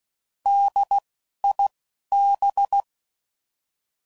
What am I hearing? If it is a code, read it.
Morse code, heard as a keyed tone, about 16 wpm, 800 Hz. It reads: D I B
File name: Prefixes / By First Letter / Ds